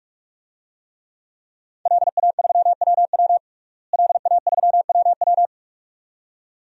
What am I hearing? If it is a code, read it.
Morse code, heard as a keyed tone, about 46 wpm, 700 Hz. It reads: LA3WW LA3WW
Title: DXCC-100 Call Signs - 5 Characters - Repeated 2x - 45wpm